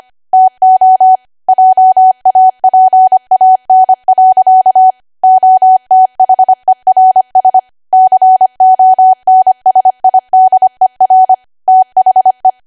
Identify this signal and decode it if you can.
Morse code, heard as a keyed tone, about 25 wpm, 750 Hz. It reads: TO JAPAN. OTHERS CONSIDER THE